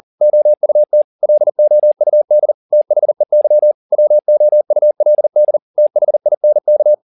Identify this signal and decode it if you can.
Morse code, heard as a keyed tone, about 40 wpm, 600 Hz. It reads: OUT LOUD THEY WOULD THINK